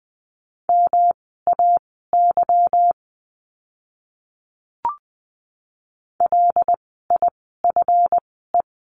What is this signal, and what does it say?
Morse code, heard as a keyed tone, about 20 wpm, spaced as for 14 wpm, 700 Hz. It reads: MAY LIFE
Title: 160 Lesson 26f - Letter g - Words 20wpm